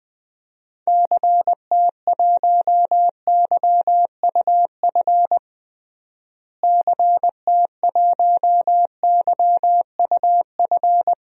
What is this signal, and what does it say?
Morse code, heard as a keyed tone, about 20 wpm, 700 Hz. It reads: CT1YUF CT1YUF
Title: DXCC-100 Call Signs - 6 Characters - Repeated 3x - 20wpm